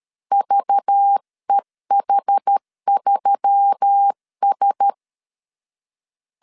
Morse code, heard as a keyed tone, about 13 wpm, 800 Hz. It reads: VEH3S
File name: CW / DET / 1 6 1500C S C 12wpm 3533